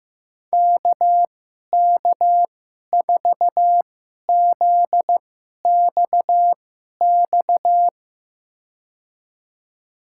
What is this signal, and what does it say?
Morse code, heard as a keyed tone, about 15 wpm, spaced as for 11 wpm, 700 Hz. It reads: KK4ZXX
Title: USA Call Signs - 6 Characters - 2x Character Spacing - 15wpm